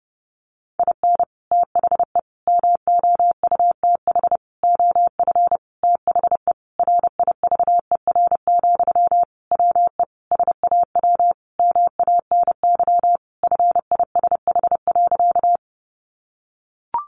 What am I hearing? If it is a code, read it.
Morse code, heard as a keyed tone, about 30 wpm, 700 Hz. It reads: IN THE MOUTH OF THE RIVER, WE SAW MANY FISH.
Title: Sentences from Top 1500 Words 30wpm